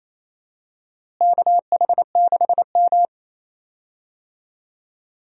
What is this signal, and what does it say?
Morse code, heard as a keyed tone, about 28 wpm, 700 Hz. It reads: KH6M
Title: DXCC-100 Call Signs - 4 Characters - Repeated 2x - 28wpm